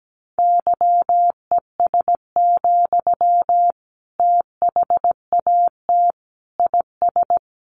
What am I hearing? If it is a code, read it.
Morse code, heard as a keyed tone, about 17 wpm, 700 Hz. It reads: YES, THAT IS
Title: Sentences from Top 500 Words 17wpm